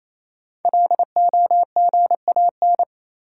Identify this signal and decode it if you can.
Morse code, heard as a keyed tone, about 28 wpm, 700 Hz. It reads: LOGAN
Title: USA Common Names - 28wpm